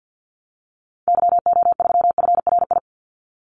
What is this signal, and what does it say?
Morse code, heard as a keyed tone, about 52 wpm, 700 Hz. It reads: YO3FRI
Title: Call Signs Repeated 3x - 50wpm